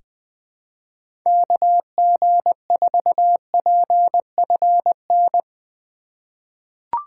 Morse code, heard as a keyed tone, about 20 wpm, 700 Hz. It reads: KG4PFN
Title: USA Call Signs - 6 Characters - 8x Character Spacing - 20wpm